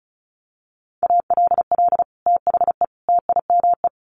Morse code, heard as a keyed tone, about 34 wpm, 700 Hz. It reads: ALL THE TIME
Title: Sets of 3 Words 35wpm